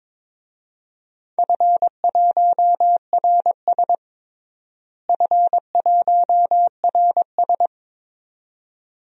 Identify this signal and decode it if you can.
Morse code, heard as a keyed tone, about 22 wpm, 700 Hz. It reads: F1RS F1RS
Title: DXCC-100 Call Signs - 4 Characters - Repeated 2x - 22wpm